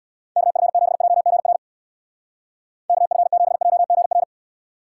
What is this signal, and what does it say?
Morse code, heard as a keyed tone, about 61 wpm, 700 Hz. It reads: KF7JGR KF7JGR
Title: Speed Racing - USA 2x3 Call Signs 50wpm